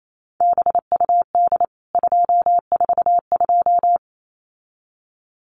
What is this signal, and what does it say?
Morse code, heard as a keyed tone, about 28 wpm, 700 Hz. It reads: BUD 242
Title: CWT Exchange 28wpm